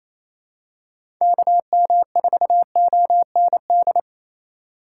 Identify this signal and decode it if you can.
Morse code, heard as a keyed tone, about 28 wpm, 700 Hz. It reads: KM4OND
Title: USA Call Signs - 4x Character Spacing - 28wpm